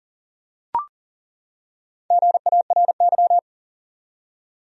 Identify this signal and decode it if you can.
Morse code, heard as a keyed tone, about 41 wpm, 700 Hz. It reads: GARY
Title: USA Common Names - Max 5 Letters - 40wpm